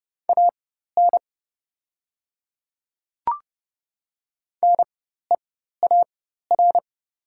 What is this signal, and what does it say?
Morse code, heard as a keyed tone, about 30 wpm, spaced as for 14 wpm, 700 Hz. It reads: AN NEAR
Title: 4x Character Spacing - Top 200 Words - 30wpm